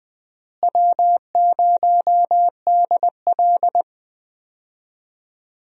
Morse code, heard as a keyed tone, about 21 wpm, 700 Hz. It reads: W0DL
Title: USA Call Signs - 4 Characters - Repeated 2x - 20wpm